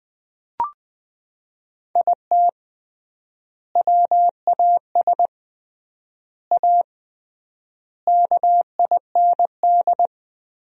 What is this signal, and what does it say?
Morse code, heard as a keyed tone, about 21 wpm, 700 Hz. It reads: IT WAS A KIND
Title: 3x Word-Spacing - Sentences from Top 500 Words - 20wpm